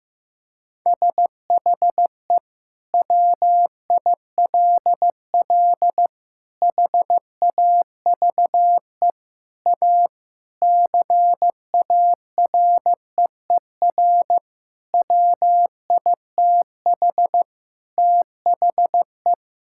Morse code, heard as a keyed tone, about 15 wpm, 700 Hz. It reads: SHE WILL HAVE A CAREER WITH THE
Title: Sentences from Top 1000 Words - Review Part 6 - 15wpm